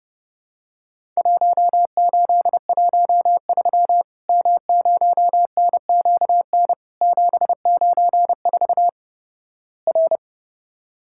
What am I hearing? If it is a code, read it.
Morse code, heard as a keyed tone, about 31 wpm, 700 Hz. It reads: 1813 M0NQN 794 R
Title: Worked All Europe DX Contest - QTC Exchange - 30wpm